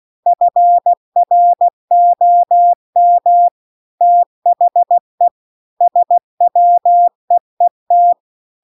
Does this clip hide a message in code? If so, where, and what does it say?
Yes: Morse code from 0.3 to 8.1 s, heard as a keyed tone, about 16 wpm, 700 Hz. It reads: FROM THE SWEET